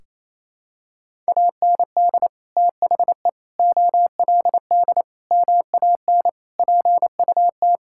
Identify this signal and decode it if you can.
Morse code, heard as a keyed tone, about 28 wpm, 700 Hz. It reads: AND THE OLD MAN PUT
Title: Sentences from Top 1600 Words 28wpm